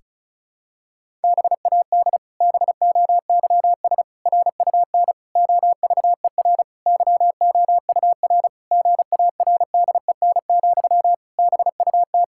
Morse code, heard as a keyed tone, about 36 wpm, 700 Hz. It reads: BAD BOYS RUN OVER YOUR GARDEN, BUT
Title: Pangram Challenge v2 - 35wpm